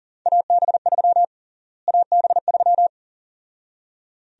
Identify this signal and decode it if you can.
Morse code, heard as a keyed tone, about 37 wpm, 700 Hz. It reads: AB3 AB3
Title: USA Call Sign Prefixes - Repeated 2x - 40wpm